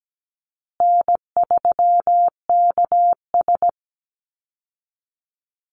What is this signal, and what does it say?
Morse code, heard as a keyed tone, about 17 wpm, 700 Hz. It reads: N3KS
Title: Easy Call Signs Repeated 3x - 17wpm